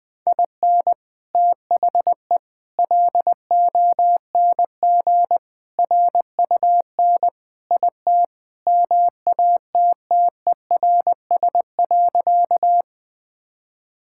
Morse code, heard as a keyed tone, about 20 wpm, 700 Hz. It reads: IN THE LONG RUN IT MATTERS.
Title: Sentences with 4-Word Phrases - Spoken Beginning - 20wpm